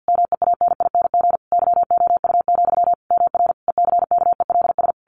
Morse code, heard as a keyed tone, about 50 wpm, 700 Hz. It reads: MEANING YOU, MR ECKELS
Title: A Sound Of Thunder 50wpm